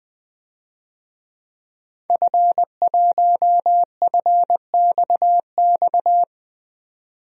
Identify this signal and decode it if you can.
Morse code, heard as a keyed tone, about 21 wpm, 700 Hz. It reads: F1FXX
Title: DXCC-100 Call Signs - 5 Characters - 20wpm